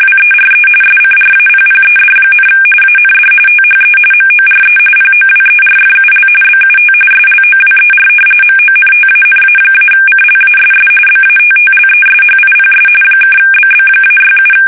RTTY, about 45.45 baud. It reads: WERE SCRAPPED, AND THE MORE POWERFUL BUT SLOWER MONTANA-CLASS BATTLESHIPS (BB-67
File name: fsk850 / battleshipnj